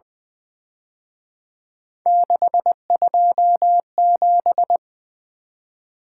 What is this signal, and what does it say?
Morse code, heard as a keyed tone, about 23 wpm, 700 Hz. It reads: 627
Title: Speed-Racing - Three Letter-Number Combinations - 20wpm